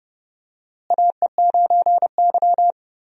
Morse code, heard as a keyed tone, about 31 wpm, 700 Hz. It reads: AE9Y
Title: USA Call Signs - 4 Characters - Repeated 2x - 30wpm